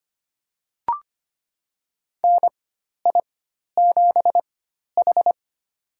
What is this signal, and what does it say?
Morse code, heard as a keyed tone, about 25 wpm, spaced as for 12 wpm, 700 Hz. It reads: NI7H
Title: USA Call Signs - 4 Characters - 4x Character Spacing - 25wpm